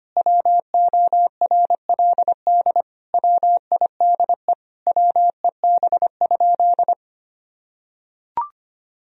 Morse code, heard as a keyed tone, about 25 wpm, 700 Hz. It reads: WORLD WIDE WEB?
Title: Sentences from Top 1000 Words - Review Part 4 - 25wpm